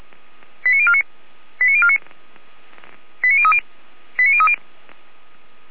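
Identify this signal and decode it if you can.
Five-tone selective call (ZVEI), heard as quick runs of five tones, tones of about 70 ms. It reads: 89049 89049 89030 89030
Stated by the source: ZVEI five-tone call received on SDR, from the Feuerwehr Orlamünde (fire brigade) audio server.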